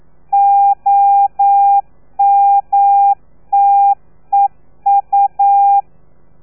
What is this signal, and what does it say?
Morse code, heard as a keyed tone, about 9 wpm, 800 Hz. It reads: OMTEU
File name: telegrafia / i2tfj / Lez3 40